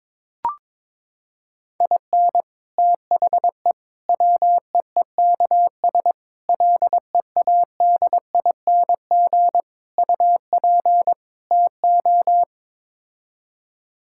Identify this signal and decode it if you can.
Morse code, heard as a keyed tone, about 21 wpm, 700 Hz. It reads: IN THE WEEKS LEADING UP TO
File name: practice / 22wpm / Sets of 6 Words - 22wpm